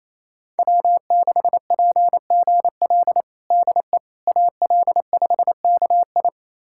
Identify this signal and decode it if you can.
Morse code, heard as a keyed tone, about 28 wpm, 700 Hz. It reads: W6PGL DE AL5KI
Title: USA Callsign DE Callsign - 5 Characters - 28wpm